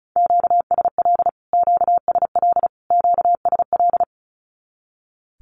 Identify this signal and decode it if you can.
Morse code, heard as a keyed tone, about 34 wpm, 700 Hz. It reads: QSL QSL QSL
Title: Q-Codes Repeated 3x 35wpm